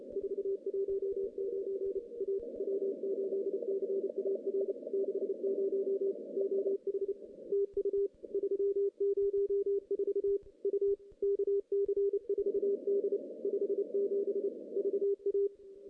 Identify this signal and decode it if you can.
Morse code, heard as a keyed tone, about 32 wpm, 370 Hz. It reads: 419 A W8CAR 60 OH TU 304 U KC4D 57 VA
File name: contests / CW-SS-k5tr- -0400